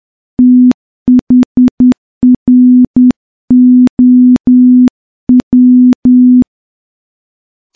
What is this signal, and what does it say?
Morse code, heard as a keyed tone, about 10 wpm, 250 Hz. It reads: THROW
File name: morse-training-10WPM